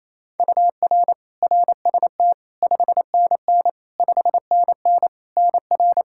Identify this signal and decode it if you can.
Morse code, heard as a keyed tone, about 28 wpm, 700 Hz. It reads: UR RST 5NN 5NN NR
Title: Practice QSO - Exchange 1 - Beginning - 28wpm